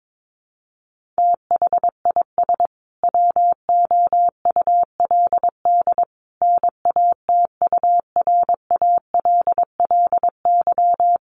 Morse code, heard as a keyed tone, about 22 wpm, 700 Hz. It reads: THIS WOULD NATURALLY
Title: Sentences from Top 1300 Words 22wpm